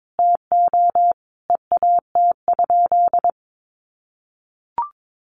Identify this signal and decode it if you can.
Morse code, heard as a keyed tone, about 22 wpm, 700 Hz. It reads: TO EAT?
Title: Sentences from Top 600 Words 22wpm